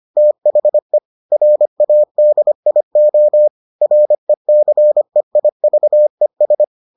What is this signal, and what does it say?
Morse code, heard as a keyed tone, about 25 wpm, 600 Hz. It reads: THE RADIO RECEIVES